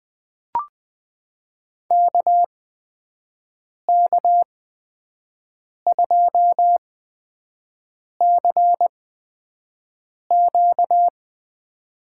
Morse code, heard as a keyed tone, about 21 wpm, spaced as for 5 wpm, 700 Hz. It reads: KK2CQ
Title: USA Call Signs - 5 Characters - 8x Character Spacing - 20wpm